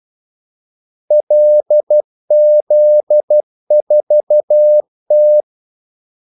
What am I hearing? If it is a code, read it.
Morse code, heard as a keyed tone, about 12 wpm, 600 Hz. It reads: LZ4T
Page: 2 ekstra ordmellemrum mellem kaldesignalerne